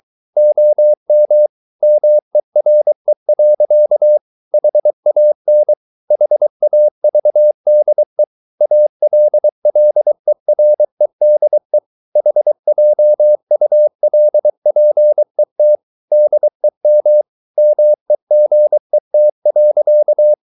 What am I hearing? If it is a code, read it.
Morse code, heard as a keyed tone, about 23 wpm, 600 Hz. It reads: OM MERE. HAN HAVDE ALLEREDE HJULPET DEM MEGET.
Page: Ferie 23 wpm